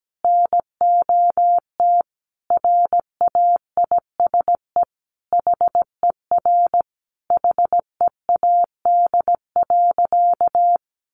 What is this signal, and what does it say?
Morse code, heard as a keyed tone, about 17 wpm, 700 Hz. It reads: NOT RAISE HER HEAD.
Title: Sentences from Top 1400 Words 17wpm